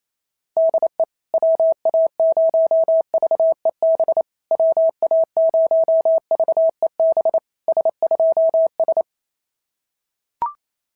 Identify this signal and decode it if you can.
Morse code, heard as a keyed tone, about 27 wpm, 650 Hz. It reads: DE WA0VEB WA0VEB S2S
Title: SOTA QSO Practice 28wpm